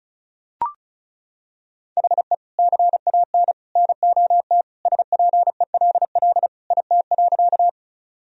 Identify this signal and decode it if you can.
Morse code, heard as a keyed tone, about 36 wpm, 700 Hz. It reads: HE CAN NOT SPELL IT.
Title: Sentences from Top 1000 Words - Review Part 1 - 35wpm